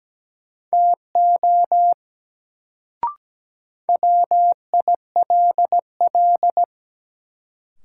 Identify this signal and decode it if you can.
Morse code, heard as a keyed tone, about 17 wpm, 700 Hz. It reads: TO WILL
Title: Sentences from Top 100 Words - New Word Drill - 17wpm